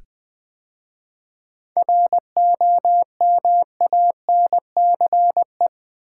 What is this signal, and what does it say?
Morse code, heard as a keyed tone, about 20 wpm, 690 Hz. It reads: ROMANCE
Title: 7 Letter Words - First 3 Letters Given - 20wpm